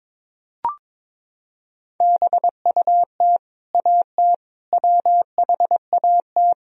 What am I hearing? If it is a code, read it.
Morse code, heard as a keyed tone, about 22 wpm, 700 Hz. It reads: BUT AT WHAT
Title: Sentences from Top 1000 Words - Review Part 4 - 22wpm